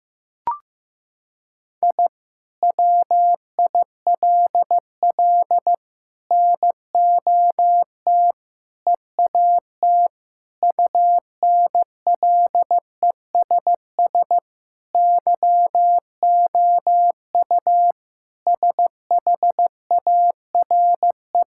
Morse code, heard as a keyed tone, about 15 wpm, 700 Hz. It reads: I WILL NOT EAT UNLESS YOU SHARE
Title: Sentences from Top 1000 Words - Review Part 6 - 15wpm